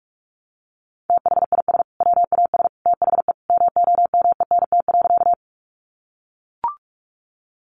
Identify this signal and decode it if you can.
Morse code, heard as a keyed tone, about 44 wpm, 700 Hz. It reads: THIS WAS THE MOMENT.
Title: Sentences from Top 1000 Words 45wpm